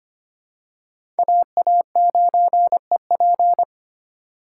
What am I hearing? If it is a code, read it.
Morse code, heard as a keyed tone, about 25 wpm, 700 Hz. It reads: AA9EP
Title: USA Call Signs - 5 Characters - 3x Character Spacing - 25wpm